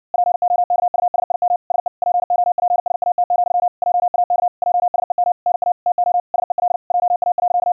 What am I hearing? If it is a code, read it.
Morse code, heard as a keyed tone, about 60 wpm, 710 Hz. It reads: POKUSIM SE POPSAT, JAK JSEM NA TO SEL JA.
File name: hamradio / clanek-qrq-70wpm / 60wpm